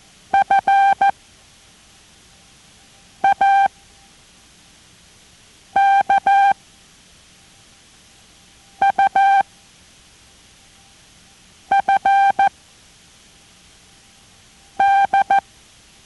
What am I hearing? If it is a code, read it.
Morse code, heard as a keyed tone, about 14 wpm, spaced as for 4 wpm, 790 Hz. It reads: FAKUFD